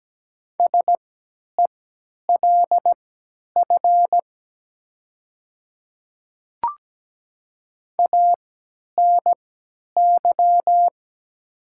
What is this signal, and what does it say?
Morse code, heard as a keyed tone, about 17 wpm, spaced as for 10 wpm, 700 Hz. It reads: SELF ANY
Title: 3x Character Spacing - Top 200 Words - 17wpm